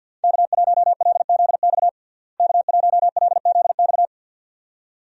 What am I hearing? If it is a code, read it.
Morse code, heard as a keyed tone, about 51 wpm, 700 Hz. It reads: K1LZX K1LZX
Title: USA Call Signs - 5 Characters - Repeated 2x - 50wpm